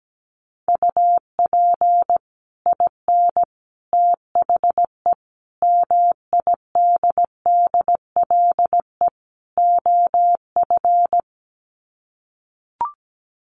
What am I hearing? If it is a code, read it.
Morse code, heard as a keyed tone, about 17 wpm, 700 Hz. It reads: UP IN THE MIDDLE OF
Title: Sets of 5 Words Encore 17wpm